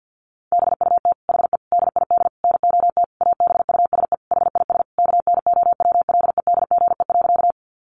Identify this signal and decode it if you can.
Morse code, heard as a keyed tone, about 50 wpm, 700 Hz. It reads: BUT HE DID NOT ABUSE HIS KNOWLEDGE.
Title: Sentences from Top 1300 Words 50wpm